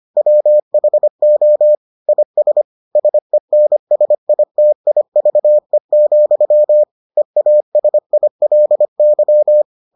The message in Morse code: WHO IS SENSITIVE, EASILY

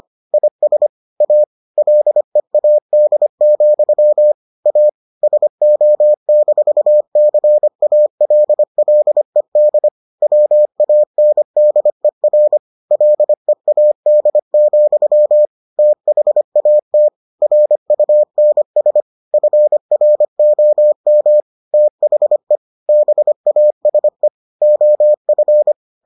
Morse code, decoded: IS A LEAD, A SO-CALLED WANDER LEAD, THAT RUNS FROM THE BASE OF

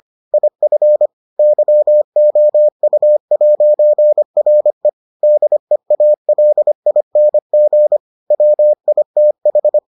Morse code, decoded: IF YOU'RE DEALING WITH

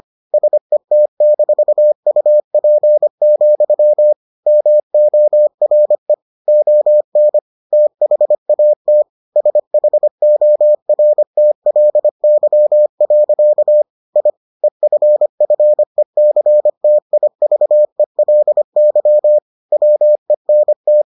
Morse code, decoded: SET-UP, MORE ON THAT SHORTLY. I EFFECTIVELY WENT

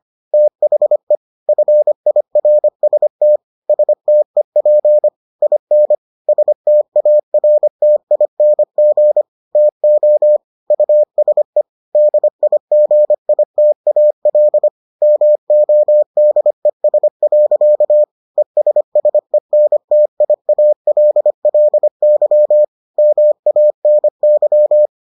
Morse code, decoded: THE FIRST STEP IN STARTING TO USE DIGITAL MODES. ESSENTIALLY MANY

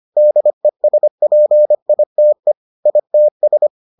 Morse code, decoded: DESPITE ITS

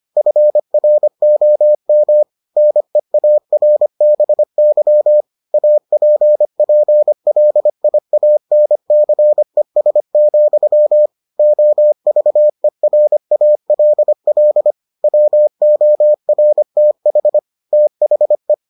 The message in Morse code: FROM NEARBY APPLIANCES, OVERALL WORTH THE

